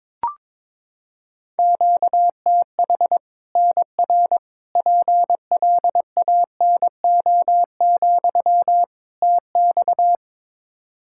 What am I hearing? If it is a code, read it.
Morse code, heard as a keyed tone, about 22 wpm, 700 Hz. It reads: QTH NR PLANO, TX
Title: Practice QSO - Exchange 1 - Beginning - 22wpm